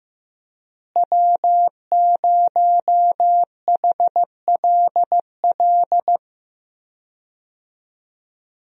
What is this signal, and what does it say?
Morse code, heard as a keyed tone, about 15 wpm, 700 Hz. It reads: W0HLL
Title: USA Call Signs - 5 Characters - Repeated 2x - 15wpm